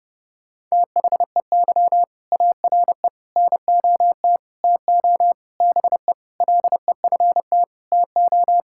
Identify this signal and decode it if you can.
Morse code, heard as a keyed tone, about 30 wpm, 700 Hz. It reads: THEY ARE NOT TO BE LEFT TO
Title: Sentences from Top 1000 Words - Review Part 5 - 30wpm